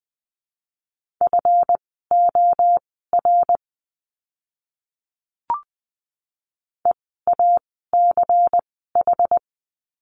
Morse code, heard as a keyed tone, about 20 wpm, spaced as for 14 wpm, 700 Hz. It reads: FOR EACH
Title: 192 Lesson 31f - Letter v - Words 20wpm